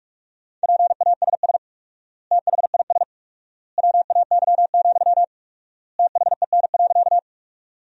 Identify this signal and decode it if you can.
Morse code, heard as a keyed tone, about 46 wpm, 700 Hz. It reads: PASS THIS WAY, THEN.
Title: 4x Word-Spacing - Sentences from Top 400 Words - 45wpm